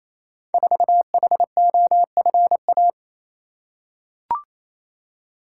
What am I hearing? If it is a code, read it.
Morse code, heard as a keyed tone, about 29 wpm, 700 Hz. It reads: 4HOFA
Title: Code Groups - 5 Letter:Numbers - 28wpm